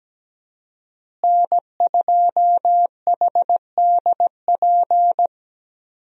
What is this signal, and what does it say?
Morse code, heard as a keyed tone, about 17 wpm, 700 Hz. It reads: N2HDP
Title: USA Call Signs - 5 Characters - Repeated 2x - 17wpm